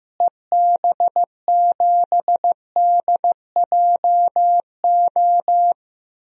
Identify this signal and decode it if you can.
Morse code, heard as a keyed tone, about 15 wpm, 700 Hz. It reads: EB7DJO
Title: DXCC-100 Call Signs - 6 Characters - Repeated 2x - 15wpm